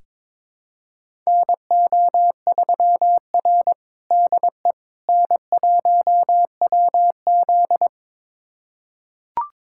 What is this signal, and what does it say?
Morse code, heard as a keyed tone, about 22 wpm, 700 Hz. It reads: NO3R DE N1WZ
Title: USA Callsign DE Callsign - 4 Characters - 22wpm